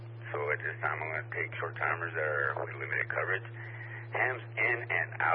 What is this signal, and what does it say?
Emergency Alert System (SAME) tones, heard as harsh buzzing bursts, with speech. It is heard from 1.0 to 5.0 s.